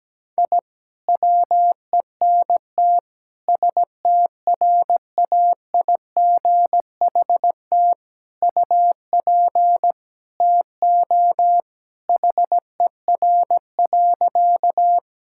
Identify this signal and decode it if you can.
Morse code, heard as a keyed tone, about 17 wpm, 700 Hz. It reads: I WENT STRAIGHT UP TO HER.
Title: Sentences from Top 1000 Words - Review Part 6 - 17wpm